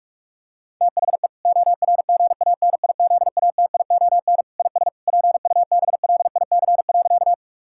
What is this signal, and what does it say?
Morse code, heard as a keyed tone, about 47 wpm, 700 Hz. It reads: THE ORGANIZATION IS PUBLIC.
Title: Sentences from Top 1000 Words - Review Part 5 - 45wpm